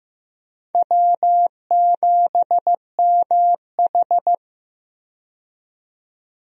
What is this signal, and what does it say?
Morse code, heard as a keyed tone, about 15 wpm, 700 Hz. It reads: W7MH
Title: USA Call Signs - 4 Characters - 15wpm